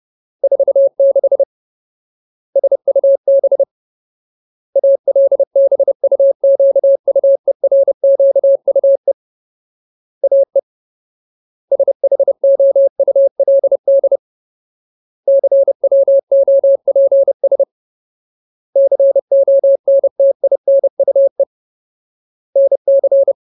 Morse code, heard as a keyed tone, about 30 wpm, 550 Hz. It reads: 46 SUB ALBUQUERQUE AE SHOULD CWOPS CONTINUE NC
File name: Ultimate Word List / 30WPM Ultimate